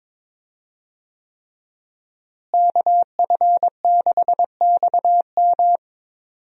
Morse code, heard as a keyed tone, about 22 wpm, 700 Hz. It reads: KF6XM
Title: USA Call Signs - 5 Characters - Repeated 3x - 22wpm